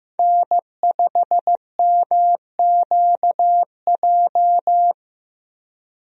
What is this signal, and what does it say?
Morse code, heard as a keyed tone, about 15 wpm, 700 Hz. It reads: N5MQJ
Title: USA Call Signs - 5 Characters - 15wpm